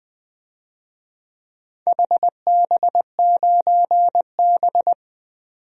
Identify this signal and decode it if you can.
Morse code, heard as a keyed tone, about 20 wpm, 700 Hz. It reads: HB9B
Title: DXCC-100 Call Signs - 4 Characters - 20wpm